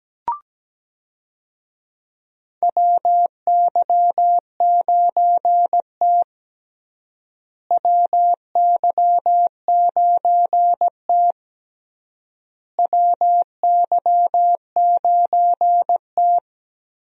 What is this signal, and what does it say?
Morse code, heard as a keyed tone, about 17 wpm, 700 Hz. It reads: WY9T WY9T WY9T
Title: USA Call Signs - 4 Characters - Repeated 3x - 17wpm